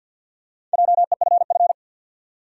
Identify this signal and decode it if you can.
Morse code, heard as a keyed tone, about 52 wpm, 700 Hz. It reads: JEFF
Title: USA Common Names - Max 4 Letters - 50wpm